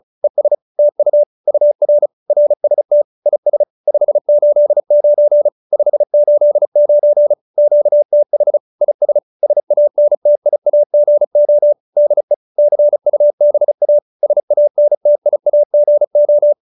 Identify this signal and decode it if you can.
Morse code, heard as a keyed tone, about 34 wpm, 590 Hz. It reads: ES TU UR RST IS 589 589 QTH IS SANTIAGO DE CUBA SANTIAGO